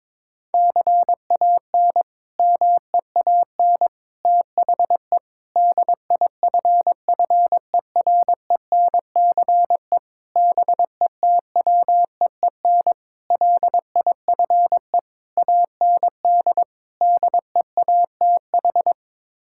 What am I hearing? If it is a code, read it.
Morse code, heard as a keyed tone, about 22 wpm, 700 Hz. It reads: CAN MEAN THE DIFFERENCE BETWEEN LIFE AND DEATH
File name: practice / 22wpm / Sets of 8 Words - 22wpm